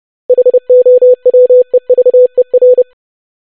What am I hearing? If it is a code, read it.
Morse code, heard as a keyed tone, about 30 wpm, 500 Hz. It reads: HOWEVER
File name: basic500-25wpm